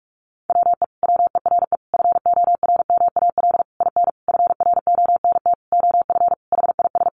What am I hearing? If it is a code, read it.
Morse code, heard as a keyed tone, about 44 wpm, 700 Hz. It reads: WE WERE FORMAL IN FRONT OF HIS
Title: Sentences from Top 1300 Words 45wpm